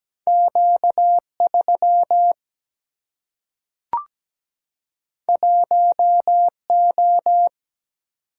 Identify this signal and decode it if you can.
Morse code, heard as a keyed tone, about 17 wpm, 700 Hz. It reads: Q3 1O
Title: Single-Letter Single-Number 17wpm